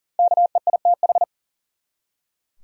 Morse code, heard as a keyed tone, about 40 wpm, 700 Hz. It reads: KEITH
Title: USA Common Names - 40wpm